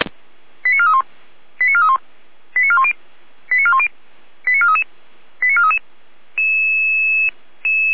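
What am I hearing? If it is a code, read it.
Five-tone selective call (ZVEI), heard as quick runs of five tones, tones of about 80 ms. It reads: 89431 89431 89410 89410 89433 89433
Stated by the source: ZVEI five-tone selective call received on SDR, fire brigade Feuerwehr Orlamünde.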